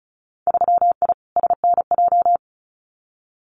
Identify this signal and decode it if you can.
Morse code, heard as a keyed tone, about 34 wpm, 700 Hz. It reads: 3I SNJ
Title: Winter Field Day Exchanges Encore 35wpm